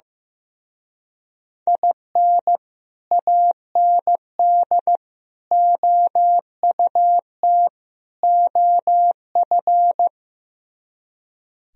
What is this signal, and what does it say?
Morse code, heard as a keyed tone, about 15 wpm, 700 Hz. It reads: IN AND OUT OF
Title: Speed-Racing - Sets of 4 Words 15wpm